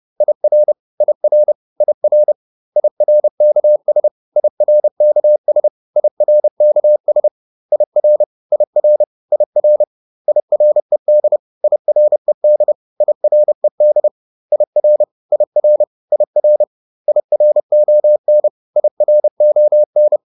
Morse code, heard as a keyed tone, about 30 wpm, 600 Hz. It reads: IR IR IR IRKS IRKS IRKS IR IR IR IRED IRED IRED IR IR IR IRON IRON